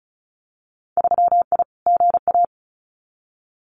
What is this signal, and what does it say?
Morse code, heard as a keyed tone, about 36 wpm, 700 Hz. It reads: 3I GA
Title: Winter Field Day Exchanges 35wpm